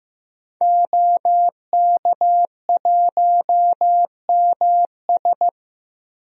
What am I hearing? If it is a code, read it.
Morse code, heard as a keyed tone, about 15 wpm, 700 Hz. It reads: OK1MS
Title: DXCC-100 Call Signs - 5 Characters - Repeated 2x - 15wpm